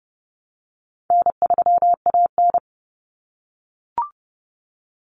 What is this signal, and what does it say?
Morse code, heard as a keyed tone, about 30 wpm, 700 Hz. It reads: N3AN
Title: Common Cut Numbers - 4-Digit 30wpm